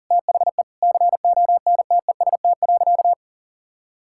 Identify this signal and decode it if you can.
Morse code, heard as a keyed tone, about 40 wpm, 700 Hz. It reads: THE CONTEST.
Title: Sentences from Top 1600 Words 40wpm